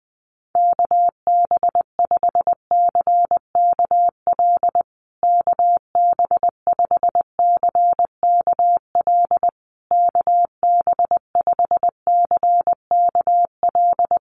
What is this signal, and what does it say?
Morse code, heard as a keyed tone, about 20 wpm, 700 Hz. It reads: KB5CKL KB5CKL KB5CKL
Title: USA Call Signs Repeated 3x 20wpm